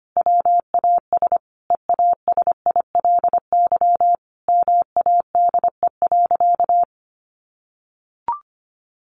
Morse code, heard as a keyed tone, about 25 wpm, 700 Hz. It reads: WAS EASILY MADE.
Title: Sentences from Top 1200 Words 25wpm